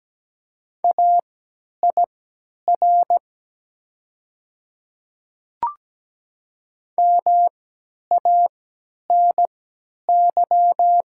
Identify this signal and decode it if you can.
Morse code, heard as a keyed tone, about 17 wpm, spaced as for 10 wpm, 700 Hz. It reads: AIR MANY
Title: 3x Character Spacing - Top 200 Words - 17wpm